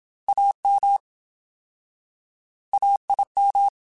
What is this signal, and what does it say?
Morse code, heard as a keyed tone, about 27 wpm, 800 Hz. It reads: AM AIM
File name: Alphabetical / Alphabetical mp3s / As @25